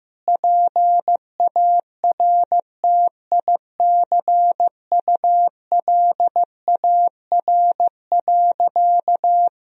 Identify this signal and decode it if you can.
Morse code, heard as a keyed tone, about 15 wpm, 700 Hz. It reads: PARTICULAR.
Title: Sentences from Top 1000 Words - Review Part 4 - 15wpm